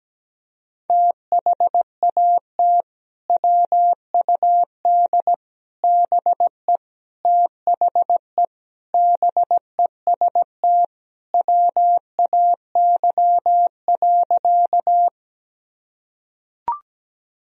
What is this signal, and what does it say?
Morse code, heard as a keyed tone, about 17 wpm, 700 Hz. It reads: THAT WUD BE THE BEST WAY.
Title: Sentences from Top 400 Words and Abbreviations 17wpm